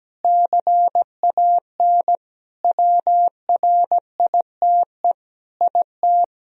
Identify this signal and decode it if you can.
Morse code, heard as a keyed tone, about 17 wpm, 700 Hz. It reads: CAN WRITE IT
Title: Sentences from Top 1000 Words - Review Part 4 - 17wpm